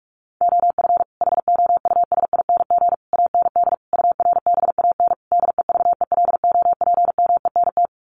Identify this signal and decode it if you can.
Morse code, heard as a keyed tone, about 44 wpm, 700 Hz. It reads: OF HOUSING AND URBAN DEVELOPMENT
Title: Sets of 5 Words Encore2 45wpm